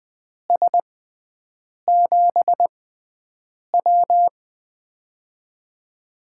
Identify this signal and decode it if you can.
Morse code, heard as a keyed tone, about 20 wpm, spaced as for 7 wpm, 700 Hz. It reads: S7W
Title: USA Call Signs - 4 Characters - 6x Character Spacing - 20wpm